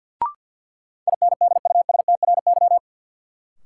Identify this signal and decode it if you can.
Morse code, heard as a keyed tone, about 53 wpm, 700 Hz. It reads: INDUSTRY
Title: Sentences from Top 1000 Words - Review Part 4 - New Word Drill - 50wpm